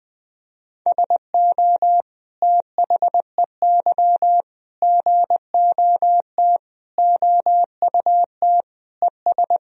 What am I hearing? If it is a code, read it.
Morse code, heard as a keyed tone, about 20 wpm, 700 Hz. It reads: SO THEY GOT OUT ES
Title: Sentences from Top 300 Words and Abbreviations 20wpm